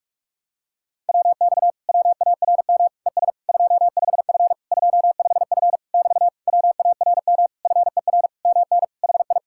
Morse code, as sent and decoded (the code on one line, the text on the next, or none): .-- -..- / .-- .- .-. -- / . ... / ..--- ..... ..-. / ..--- ..... ..-. / -...- / .-- .- .-. -- / ..-. .-. / -- -. / .... ..
WX WARM ES 25F 25F <BT> WARM FER MN HI